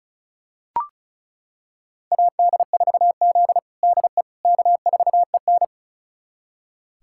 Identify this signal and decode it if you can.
Morse code, heard as a keyed tone, about 36 wpm, 700 Hz. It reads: AD4Z DE K4EN
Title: USA Callsign DE Callsign - 4 Characters - 35wpm